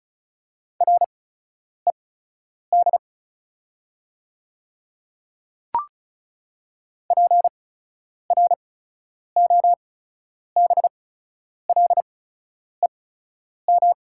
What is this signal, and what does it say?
Morse code, heard as a keyed tone, about 36 wpm, spaced as for 10 wpm, 700 Hz. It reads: RED PROBLEM
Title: 8x Character Spacing - Top 500 Words - 35wpm